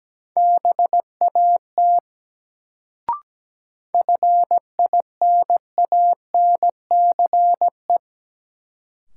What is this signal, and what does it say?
Morse code, heard as a keyed tone, about 17 wpm, 700 Hz. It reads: BAT FINANCE
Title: Sentences from Top 1500 Words - New Word Drill Alt - 17wpm